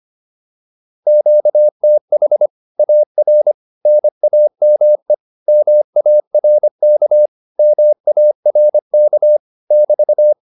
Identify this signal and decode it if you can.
Morse code, heard as a keyed tone, about 25 wpm, 600 Hz. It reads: QTH AR NAME MARK MARK <BT>
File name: QSO210a 25